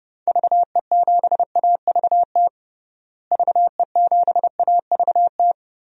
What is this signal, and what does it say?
Morse code, heard as a keyed tone, about 30 wpm, 700 Hz. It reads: VE7AVT VE7AVT
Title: DXCC-100 Call Signs - 6 Characters - Repeated 3x - 30wpm